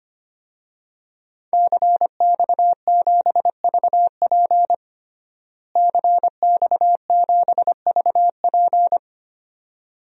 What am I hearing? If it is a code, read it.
Morse code, heard as a keyed tone, about 25 wpm, 700 Hz. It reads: CX7VP CX7VP
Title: DXCC-100 Call Signs - 5 Characters - Repeated 2x - 25wpm